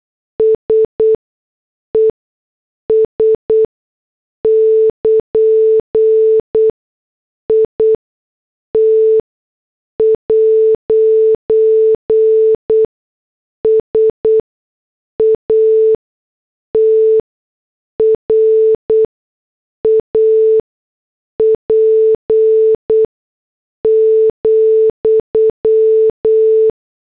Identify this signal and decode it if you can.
Morse code, heard as a keyed tone, about 8 wpm, spaced as for 6 wpm, 440 Hz. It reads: SES <KN> IT'S A TRAP,